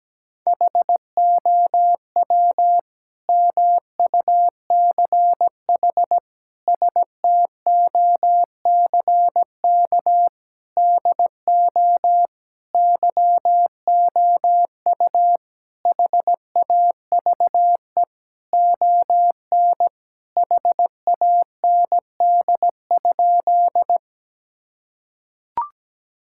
Morse code, heard as a keyed tone, about 17 wpm, 700 Hz. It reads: HOW MUCH STOCK DO YOU HAVE ON HAND?
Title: Sentences from Top 1000 Words - Review Part 5 - 17wpm